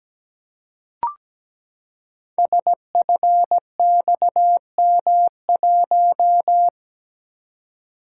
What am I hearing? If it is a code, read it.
Morse code, heard as a keyed tone, about 17 wpm, 700 Hz. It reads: SFXM1
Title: Code Groups - 5 Letter:Numbers - 17wpm